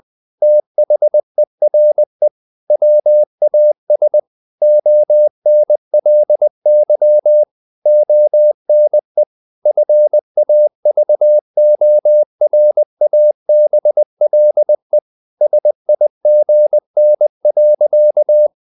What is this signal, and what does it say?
Morse code, heard as a keyed tone, about 20 wpm, 600 Hz. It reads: THERE WAS ONLY ONE FAVORABLE SIGN.
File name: the old man and the sea / the old man and the sea part 1 20wpm